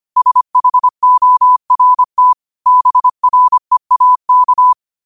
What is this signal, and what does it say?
Morse code, heard as a keyed tone, about 25 wpm, 1000 Hz. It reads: SHORT BREAK
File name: qotdfiles / 25wpm